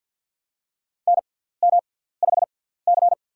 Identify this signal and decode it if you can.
Morse code, heard as a keyed tone, about 53 wpm, spaced as for 17 wpm, 700 Hz. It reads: NM5C
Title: USA Call Signs - 4 Characters - 6x Character Spacing - 50wpm